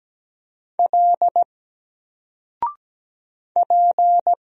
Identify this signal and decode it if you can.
Morse code, heard as a keyed tone, about 17 wpm, 700 Hz. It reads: L P
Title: Single Letter-Number 17wpm